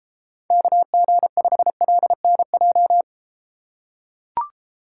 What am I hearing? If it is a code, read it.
Morse code, heard as a keyed tone, about 33 wpm, 700 Hz. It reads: KG5LNJ